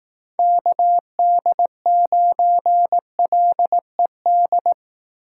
Morse code, heard as a keyed tone, about 18 wpm, 700 Hz. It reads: KD9LED
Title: Speed Racing - USA Call Signs 15wpm